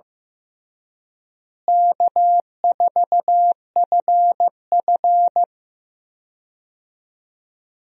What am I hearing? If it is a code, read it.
Morse code, heard as a keyed tone, about 16 wpm, 700 Hz. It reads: K4FF
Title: USA Call Signs - 4 Characters - Repeated 3x - 15wpm